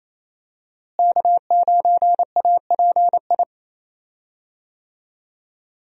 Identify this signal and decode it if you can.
Morse code, heard as a keyed tone, about 28 wpm, 700 Hz. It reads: K9API
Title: USA Call Signs - 5 Characters - Repeated 3x - 28wpm